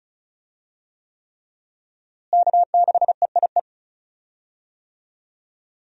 Morse code, heard as a keyed tone, about 36 wpm, 700 Hz. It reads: K6EIE
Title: USA Call Signs - 5 Characters - Repeated 3x - 35wpm